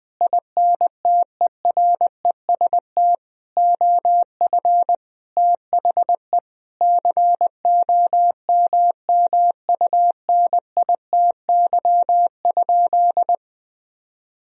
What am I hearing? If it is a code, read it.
Morse code, heard as a keyed tone, about 20 wpm, 700 Hz. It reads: INTEREST OF THE COMMUNITY?
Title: Sentences from Top 1000 Words - Review Part 4 - 20wpm